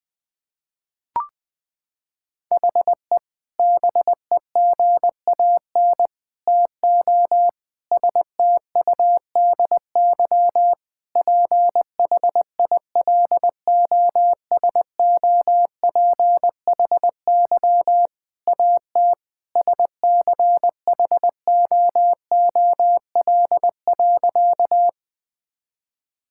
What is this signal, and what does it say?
Morse code, heard as a keyed tone, about 20 wpm, 700 Hz. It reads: HE BEGAN TO STUDY PHILOSOPHY AT SCHOOL.
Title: Sentences from Top 1000 Words - Review Part 6 - 20wpm